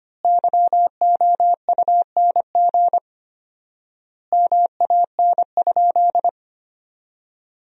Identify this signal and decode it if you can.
Morse code, heard as a keyed tone, about 25 wpm, 700 Hz. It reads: YOUNG MAN?
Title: 4x Word-Spacing - Sentences from Top 400 Words - 25wpm